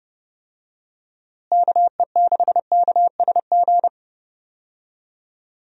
Morse code, heard as a keyed tone, about 30 wpm, 700 Hz. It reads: KE6KSG
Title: USA Call Signs - 6 Characters - Repeated 2x - 30wpm